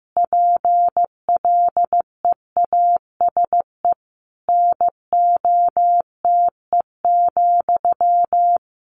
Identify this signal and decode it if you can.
Morse code, heard as a keyed tone, about 15 wpm, 700 Hz. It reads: PLEASE NOTE,
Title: A Sound Of Thunder 15wpm